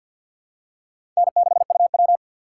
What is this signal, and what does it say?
Morse code, heard as a keyed tone, about 52 wpm, 700 Hz. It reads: N6UW
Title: USA Call Signs - 4 Characters - 2x Character Spacing - 50wpm